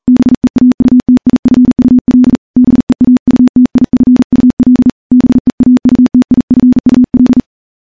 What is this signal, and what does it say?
Morse code, heard as a keyed tone, about 40 wpm, 250 Hz. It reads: BEAUTIFUL BEAUTIFUL BEAUTIFUL
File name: morse-training-40WPM